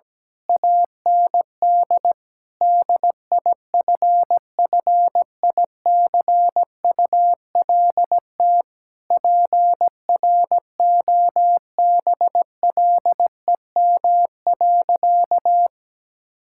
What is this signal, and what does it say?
Morse code, heard as a keyed tone, about 17 wpm, 700 Hz. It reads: AND DIFFICULT PROBLEM.
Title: Sentences from Top 1000 Words - Review Part 6 - 17wpm